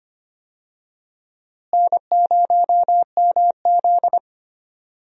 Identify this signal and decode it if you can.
Morse code, heard as a keyed tone, about 25 wpm, 700 Hz. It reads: N0MZ
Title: USA Call Signs - 4 Characters - Repeated 2x - 25wpm